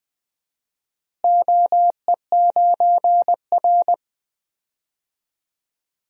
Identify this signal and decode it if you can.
Morse code, heard as a keyed tone, about 21 wpm, 700 Hz. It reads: OE9R
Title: DXCC-100 Call Signs - 4 Characters - Repeated 3x - 20wpm